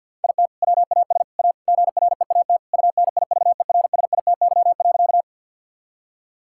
Morse code, heard as a keyed tone, about 50 wpm, 700 Hz. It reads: IT WAS A GREAT UNIVERSITY.
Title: Sentences from Top 1000 Words - Review Part 5 - 50wpm